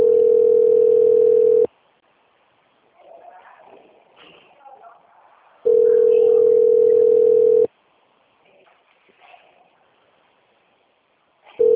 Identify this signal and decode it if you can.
Ringback tone (North America), 440 + 480 Hz, with speech.
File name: out- - - - - .93913